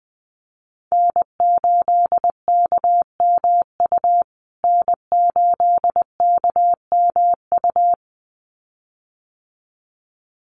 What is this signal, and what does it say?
Morse code, heard as a keyed tone, about 19 wpm, 700 Hz. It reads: N8KMU N8KMU
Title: USA Call Signs Repeated 2x 20wpm